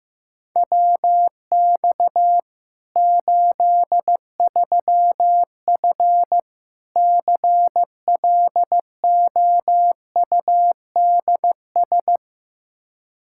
Morse code, heard as a keyed tone, about 15 wpm, 700 Hz. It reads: WX 83F CLOUDS
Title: Practice QSO - Exchange 2 - Info - 15wpm